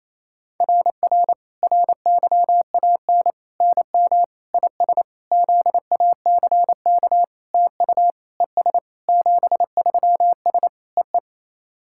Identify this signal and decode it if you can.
Morse code, heard as a keyed tone, about 28 wpm, 700 Hz. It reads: RR RYAN NM IS ZACK TU ES 73S EE
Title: SOTA QSO Practice 28wpm